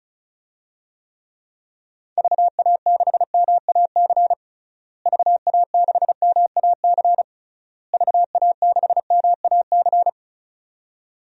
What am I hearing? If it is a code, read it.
Morse code, heard as a keyed tone, about 36 wpm, 700 Hz. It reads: VA6MAC VA6MAC VA6MAC
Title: DXCC-100 Call Signs - 6 Characters - Repeated 3x - 35wpm